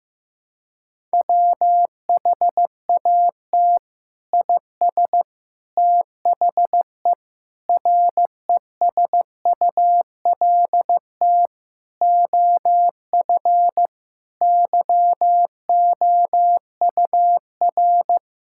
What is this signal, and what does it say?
Morse code, heard as a keyed tone, about 15 wpm, 700 Hz. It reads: WHAT IS THE RESULT OF YOUR
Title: Sentences from Top 1000 Words - Review Part 5 - 15wpm